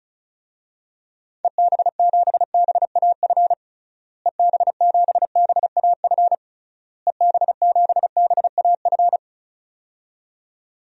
Morse code, heard as a keyed tone, about 34 wpm, 700 Hz. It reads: EB7BAF EB7BAF EB7BAF
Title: DXCC-100 Call Signs - 6 Characters - Repeated 3x - 35wpm